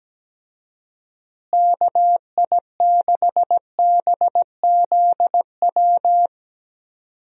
Morse code, heard as a keyed tone, about 17 wpm, 700 Hz. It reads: KI6BZW
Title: USA Call Signs - 6 Characters - Repeated 3x - 17wpm